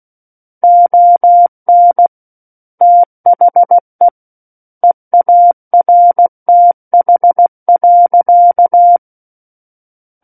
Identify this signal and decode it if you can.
Morse code, heard as a keyed tone, about 16 wpm, 700 Hz. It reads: ON THE EARTH.
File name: Book One / 07-Chapter 7 - 16 WPM